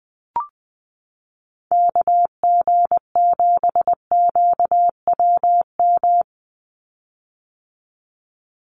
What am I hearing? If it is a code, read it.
Morse code, heard as a keyed tone, about 20 wpm, 700 Hz. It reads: KG7QWM
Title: USA Call Signs 20wpm